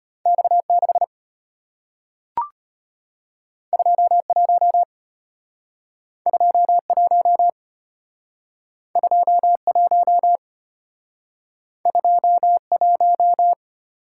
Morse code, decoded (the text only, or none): X6 21 21 21 21